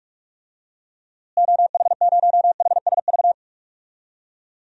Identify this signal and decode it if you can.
Morse code, heard as a keyed tone, about 46 wpm, 700 Hz. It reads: OH0HSV
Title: DXCC-100 Call Signs - 6 Characters - Repeated 2x - 45wpm